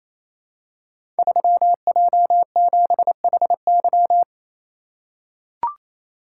Morse code, heard as a keyed tone, about 27 wpm, 700 Hz. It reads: 3J7HY
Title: Code Groups - 5 Letter:Numbers - 28wpm